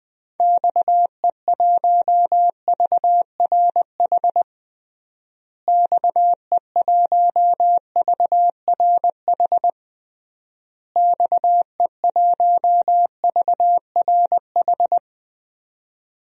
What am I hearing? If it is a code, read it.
Morse code, heard as a keyed tone, about 20 wpm, 700 Hz. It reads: XE1VRH XE1VRH XE1VRH
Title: DXCC-100 Call Signs - 6 Characters - Repeated 3x - 20wpm